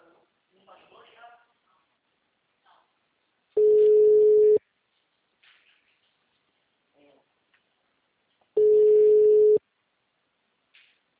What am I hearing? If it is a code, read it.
Ringback tone (Europe, 425 Hz), with speech.